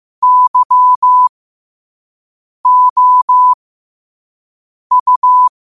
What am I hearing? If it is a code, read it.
Morse code, heard as a keyed tone, about 15 wpm, spaced as for 5 wpm, 1000 Hz. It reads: YOU
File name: qotdfiles / 5wpm